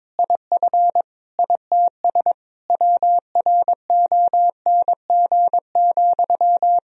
Morse code, heard as a keyed tone, about 22 wpm, 700 Hz. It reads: IF ITS WRONG,
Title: Sentences from Top 1000 Words - Review Part 3 - 22wpm